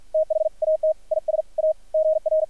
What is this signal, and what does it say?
Morse code, heard as a keyed tone, about 53 wpm, spaced as for 46 wpm, 590 Hz, with radio noise. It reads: THAT IS A GR
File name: archived / CRSnet